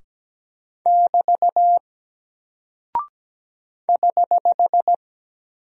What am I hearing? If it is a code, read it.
Morse code, heard as a keyed tone, about 17 wpm, 700 Hz. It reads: <BT> <HH>
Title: Single Character (Letter:Number:Punctuation:Prosign) 17wpm